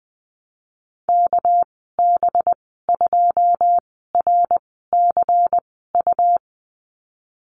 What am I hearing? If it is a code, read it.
Morse code, heard as a keyed tone, about 19 wpm, spaced as for 14 wpm, 700 Hz. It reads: KB2RCU
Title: 213 Lesson 34g - Number 8 - Call Signs 20wpm